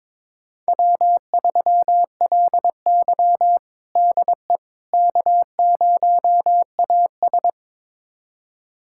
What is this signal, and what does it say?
Morse code, heard as a keyed tone, about 23 wpm, 700 Hz. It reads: W3LY DE K0AS
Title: USA Callsign DE Callsign - 4 Characters - 22wpm